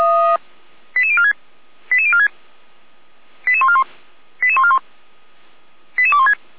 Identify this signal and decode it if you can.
Five-tone selective call (ZVEI), heard as quick runs of five tones, tones of about 70 ms. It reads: 88046 88046 88141 88141 88216